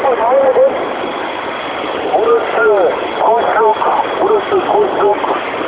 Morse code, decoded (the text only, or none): none